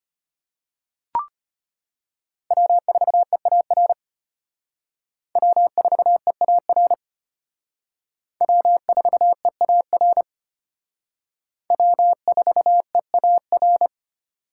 Morse code, heard as a keyed tone, about 33 wpm, 700 Hz. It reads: W4EAR W4EAR W4EAR W4EAR